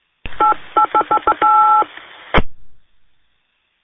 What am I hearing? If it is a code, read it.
DTMF, heard as telephone touch tones, tones of about 90 ms. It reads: 888888
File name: qsorec SimplexLogic